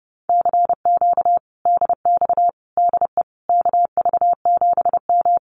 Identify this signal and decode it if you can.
Morse code, heard as a keyed tone, about 30 wpm, 700 Hz. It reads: CQ DX DE KV7M
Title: Common QSO Phrases Encore 30wpm